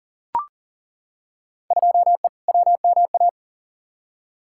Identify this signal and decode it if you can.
Morse code, heard as a keyed tone, about 40 wpm, 700 Hz. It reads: 2E WMA
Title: Field Day Exchanges Encore - 40wpm